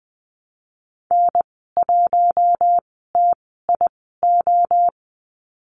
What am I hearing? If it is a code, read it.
Morse code, heard as a keyed tone, about 20 wpm, spaced as for 15 wpm, 700 Hz. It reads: N1TIO
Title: 048 Lesson 9g - Number 4 - Call Signs 20wpm